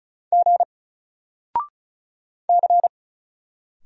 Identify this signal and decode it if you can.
Morse code, heard as a keyed tone, about 36 wpm, 700 Hz. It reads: G C
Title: Single Letter-Number 35wpm